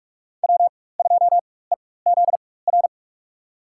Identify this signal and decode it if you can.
Morse code, heard as a keyed tone, about 46 wpm, spaced as for 21 wpm, 700 Hz. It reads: W2EZR
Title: USA Call Signs - 5 Characters - 4x Character Spacing - 45wpm